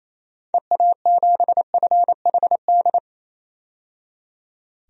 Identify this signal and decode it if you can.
Morse code, heard as a keyed tone, about 28 wpm, 700 Hz. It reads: EA7FHD
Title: DXCC-100 Call Signs - 6 Characters - 28wpm